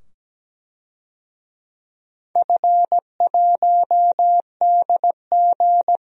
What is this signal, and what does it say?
Morse code, heard as a keyed tone, about 17 wpm, 700 Hz. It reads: F1DG
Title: DXCC-100 Call Signs - 4 Characters - 17wpm